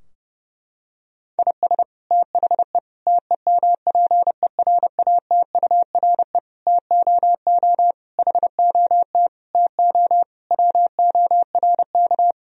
Morse code, heard as a keyed tone, about 32 wpm, 700 Hz. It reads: IS THE TEMPERATURE TOO HOT TO WORK
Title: Sentences from Top 1000 Words - Review Part 4 - 30wpm